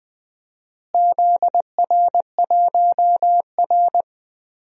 Morse code, heard as a keyed tone, about 20 wpm, 700 Hz. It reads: ZR1R
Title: DXCC-100 Call Signs - 4 Characters - Repeated 3x - 20wpm